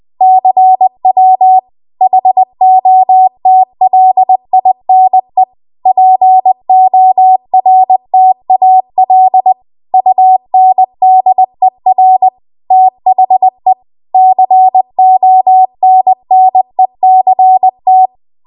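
Morse code, heard as a keyed tone, about 20 wpm, 750 Hz. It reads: CW HOTLINE PORTAL UNDER THE CONNECT
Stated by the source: Morse code practice at 20 WPM.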